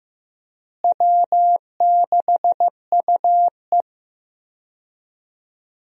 Morse code, heard as a keyed tone, about 15 wpm, 700 Hz. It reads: W6UE
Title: USA Call Signs - 4 Characters - Repeated 3x - 15wpm